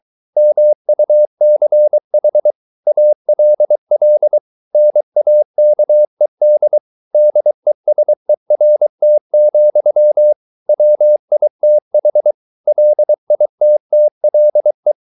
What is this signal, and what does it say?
Morse code, heard as a keyed tone, about 23 wpm, 600 Hz. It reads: MUCH ALL NAKED DESERT, WITH LITTLE